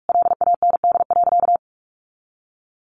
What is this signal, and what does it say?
Morse code, heard as a keyed tone, about 44 wpm, 700 Hz. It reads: LAND.
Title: Sentences from Top 1100 Words 45wpm